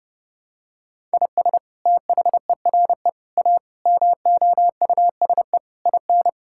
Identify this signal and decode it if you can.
Morse code, heard as a keyed tone, about 32 wpm, 700 Hz. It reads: IS THERE A MOUSE IN
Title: Sentences from Top 1000 Words - Review Part 5 - 30wpm